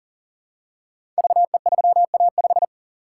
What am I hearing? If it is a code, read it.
Morse code, heard as a keyed tone, about 39 wpm, 700 Hz. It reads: VE3A5
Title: Code Groups - 5 Letter:Numbers - 40wpm